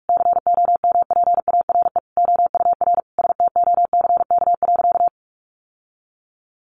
Morse code, heard as a keyed tone, about 44 wpm, 700 Hz. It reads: COMPARE OUR STOCK.
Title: Sentences from Top 1300 Words 45wpm